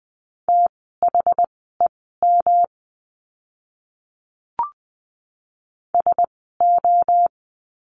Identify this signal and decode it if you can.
Morse code, heard as a keyed tone, about 20 wpm, spaced as for 14 wpm, 700 Hz. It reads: THEM SO
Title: 108 Lesson 18f - Letter m - Words 20wpm